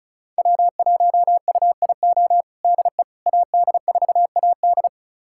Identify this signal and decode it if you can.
Morse code, heard as a keyed tone, about 34 wpm, 700 Hz. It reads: W1UIO DE AD4AD
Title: USA Callsign DE Callsign - 5 Characters - 35wpm